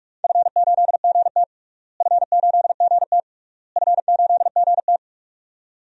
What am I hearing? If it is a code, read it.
Morse code, heard as a keyed tone, about 46 wpm, 700 Hz. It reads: F8GT F8GT F8GT
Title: DXCC-100 Call Signs - 4 Characters - Repeated 3x - 45wpm